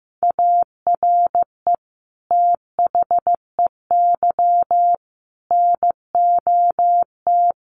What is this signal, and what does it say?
Morse code, heard as a keyed tone, about 15 wpm, 700 Hz. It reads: ARE THEY NOT
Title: Sentences from Top 1200 Words 15wpm